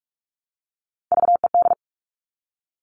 Morse code, heard as a keyed tone, about 44 wpm, 700 Hz. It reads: VED
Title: All Cut Numbers - 3-Digit 45wpm